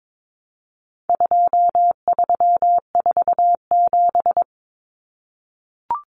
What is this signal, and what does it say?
Morse code, heard as a keyed tone, about 22 wpm, 700 Hz. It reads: 2347
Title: Numbers - 4-Digit 22wpm